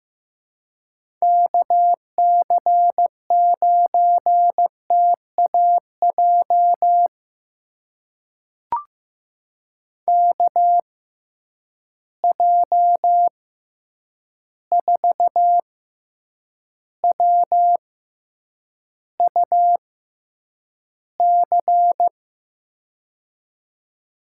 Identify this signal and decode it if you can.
Morse code, heard as a keyed tone, about 15 wpm, spaced as for 8 wpm, 700 Hz. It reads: KC9TAJ KJ4WUC
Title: USA Call Signs - 6 Characters - 6x Character Spacing - 15wpm